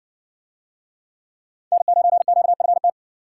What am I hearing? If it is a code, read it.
Morse code, heard as a keyed tone, about 63 wpm, 700 Hz. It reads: N9QFT